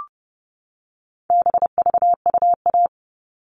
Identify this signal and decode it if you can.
Morse code, heard as a keyed tone, about 30 wpm, 700 Hz. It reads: BVUA
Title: All Cut Numbers - 4-Digit 30wpm